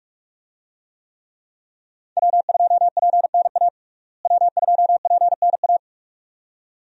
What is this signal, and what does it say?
Morse code, heard as a keyed tone, about 43 wpm, 700 Hz. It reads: W2PNA W2PNA
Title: USA Call Signs - 5 Characters - Repeated 2x - 45wpm